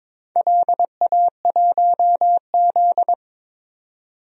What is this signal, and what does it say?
Morse code, heard as a keyed tone, about 22 wpm, 700 Hz. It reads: LA1Z
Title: DXCC-100 Call Signs - 4 Characters - 22wpm